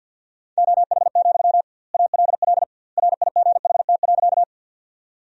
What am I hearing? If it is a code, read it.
Morse code, heard as a keyed tone, about 51 wpm, 700 Hz. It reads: OH, ALL RIGHT.
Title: 2x Word-Spacing - Sentences from Top 500 Words - 50wpm